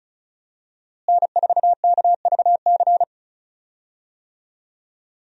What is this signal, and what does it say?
Morse code, heard as a keyed tone, about 34 wpm, 700 Hz. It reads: N4KVC
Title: USA Call Signs - 5 Characters - Repeated 3x - 35wpm